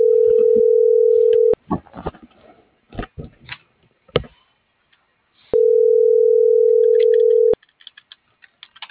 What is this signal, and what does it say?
Ringback tone (North America), 440 + 480 Hz.